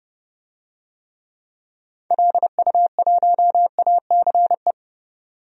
Morse code, heard as a keyed tone, about 31 wpm, 700 Hz. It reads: LU1ACE
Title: DXCC-100 Call Signs - 6 Characters - 30wpm